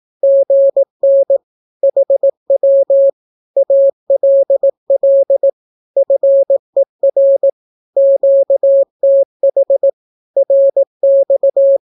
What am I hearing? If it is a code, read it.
Morse code, heard as a keyed tone, about 18 wpm, 550 Hz. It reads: GN HW ALL FER QTH RX
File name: naukapodstaw / skroty / skroty